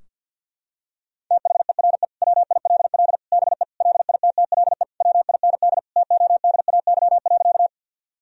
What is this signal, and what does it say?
Morse code, heard as a keyed tone, about 47 wpm, 700 Hz. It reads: THERE WILL BE LITTLE WIND TODAY.
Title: Sentences from Top 1000 Words - Review Part 2 - 50wpm